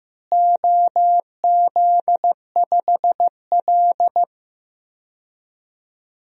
Morse code, heard as a keyed tone, about 15 wpm, 700 Hz. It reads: OZ5L
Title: DXCC-100 Call Signs - 4 Characters - Repeated 2x - 15wpm